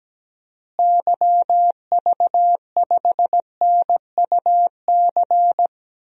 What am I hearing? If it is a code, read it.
Morse code, heard as a keyed tone, about 17 wpm, 700 Hz. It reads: YV5NUC
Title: DXCC-100 Call Signs - 6 Characters - Repeated 2x - 17wpm